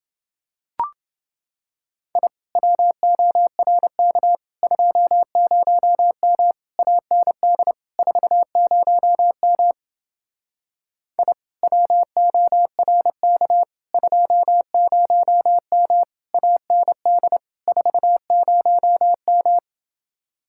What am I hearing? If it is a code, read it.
Morse code, heard as a keyed tone, about 29 wpm, 700 Hz. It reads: I WORK 20M AND 40M I WORK 20M AND 40M